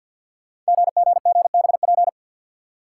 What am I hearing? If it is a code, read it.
Morse code, heard as a keyed tone, about 46 wpm, 700 Hz. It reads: GGGBP
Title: Code Groups - 5 Letters - 50wpm